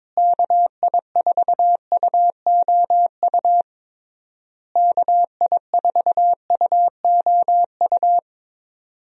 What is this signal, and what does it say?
Morse code, heard as a keyed tone, about 22 wpm, 700 Hz. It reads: KI4UOU KI4UOU
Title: USA Call Signs - 6 Characters - Repeated 3x - 22wpm